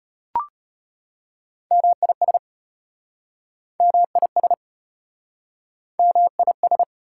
Morse code, heard as a keyed tone, about 35 wpm, 700 Hz. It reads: MIS MIS MIS